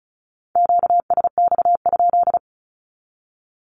Morse code, heard as a keyed tone, about 34 wpm, 700 Hz. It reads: QSX?
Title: Q-Codes 35wpm